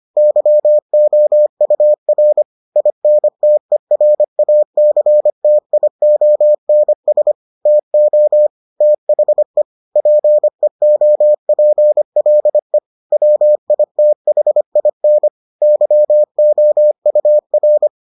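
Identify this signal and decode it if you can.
Morse code, heard as a keyed tone, about 25 wpm, 600 Hz. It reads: YOUR INTERACTIONS TO THE PEOPLE WITHIN YOUR
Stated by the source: Morse edition of the Foundations of Amateur Radio episodes.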